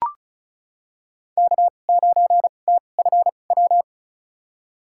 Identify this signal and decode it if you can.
Morse code, heard as a keyed tone, about 34 wpm, spaced as for 25 wpm, 700 Hz. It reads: K9TFW
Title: USA Call Signs - 5 Characters - 2x Character Spacing - 35wpm